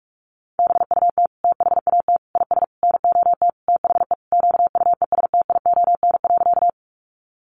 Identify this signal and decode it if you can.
Morse code, heard as a keyed tone, about 44 wpm, 700 Hz. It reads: BUT THAT IS NOT THE QUESTION.
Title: Sentences from Top 600 Words 45wpm